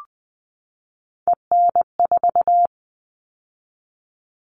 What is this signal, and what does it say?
Morse code, heard as a keyed tone, about 20 wpm, 700 Hz. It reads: EN4
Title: All Cut Numbers - 3-Digit 20wpm